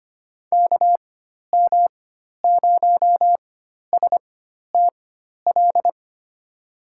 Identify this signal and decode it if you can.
Morse code, heard as a keyed tone, about 25 wpm, spaced as for 12 wpm, 700 Hz. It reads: KM0STL
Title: USA Call Signs - 6 Characters - 4x Character Spacing - 25wpm